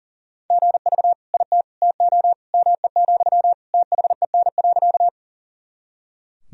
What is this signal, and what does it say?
Morse code, heard as a keyed tone, about 40 wpm, 700 Hz. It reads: GV IT TO ME, THEN.
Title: Sentences from Top 200 Words and Abbreviations 40wpm